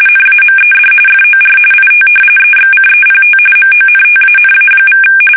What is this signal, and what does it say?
RTTY, about 45.45 baud. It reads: NAMED FOR IT EXCEPT MONTANA=
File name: fsk850 / battleships